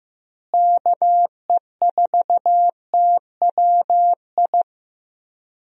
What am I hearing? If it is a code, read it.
Morse code, heard as a keyed tone, about 15 wpm, 700 Hz. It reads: KE4TWI
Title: USA Call Signs - 6 Characters - Repeated 3x - 15wpm